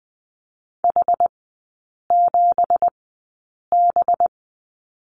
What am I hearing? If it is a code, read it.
Morse code, heard as a keyed tone, about 20 wpm, spaced as for 8 wpm, 700 Hz. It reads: H7B
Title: 191 Lesson 31e - Letter v - Three Characters 20wpm